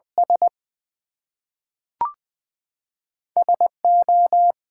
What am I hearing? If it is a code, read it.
Morse code, heard as a keyed tone, about 20 wpm, 700 Hz. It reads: S SO
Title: Two Letter Words - 20wpm